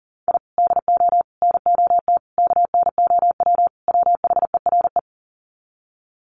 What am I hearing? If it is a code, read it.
Morse code, heard as a keyed tone, about 40 wpm, 700 Hz. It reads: I DO NOT KNOW WHERE
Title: Sets of 5 Words 40wpm